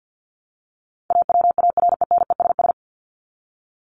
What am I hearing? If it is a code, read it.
Morse code, heard as a keyed tone, about 50 wpm, 700 Hz. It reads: AWARENESS
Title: Top 1501-1750 Words 50wpm